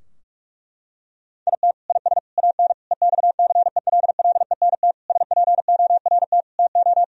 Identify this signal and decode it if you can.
Morse code, heard as a keyed tone, about 46 wpm, 700 Hz. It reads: IT IS AN EXCELLENT SPORT TO
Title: Sentences from Top 1000 Words - Review Part 4 - 45wpm